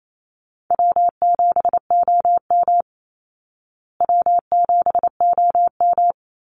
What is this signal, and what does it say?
Morse code, heard as a keyed tone, about 28 wpm, 700 Hz. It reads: W7OM W7OM
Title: Easy Call Signs Repeated 2x - 28wpm